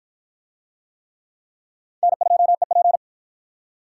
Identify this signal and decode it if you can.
Morse code, heard as a keyed tone, about 47 wpm, 700 Hz. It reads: N2EP